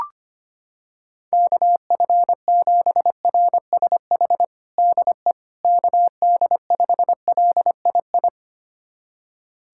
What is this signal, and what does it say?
Morse code, heard as a keyed tone, about 25 wpm, 700 Hz. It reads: KF7RSH DE KD5LII
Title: USA Callsign DE Callsign 25wpm